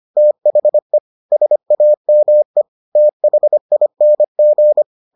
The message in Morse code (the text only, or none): THE SAME THING